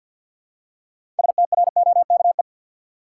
Morse code, heard as a keyed tone, about 53 wpm, 700 Hz. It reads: STROKE
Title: Six Letter Words - 50wpm